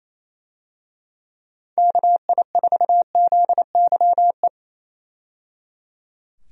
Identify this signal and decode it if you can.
Morse code, heard as a keyed tone, about 27 wpm, 700 Hz. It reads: KI4ZYE
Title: USA Call Signs - 6 Characters - 28wpm